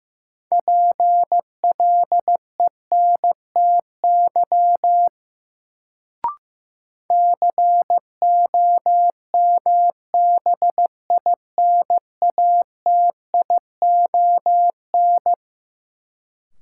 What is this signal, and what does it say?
Morse code, heard as a keyed tone, about 15 wpm, 700 Hz. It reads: PLENTY COMBINATION
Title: Sentences from Top 1300 Words - New Word Drill Alt - 15wpm